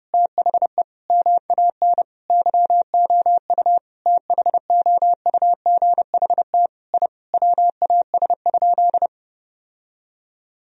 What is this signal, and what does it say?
Morse code, heard as a keyed tone, about 30 wpm, 700 Hz. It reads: THE MAN YOU THOUGHT I WAS?
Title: Sentences from Top 1000 Words - Review Part 3 - 30wpm